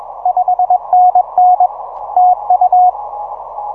Morse code, heard as a keyed tone, about 21 wpm, 730 Hz. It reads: 5NN TU
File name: simple-file-list / UN7TW 6M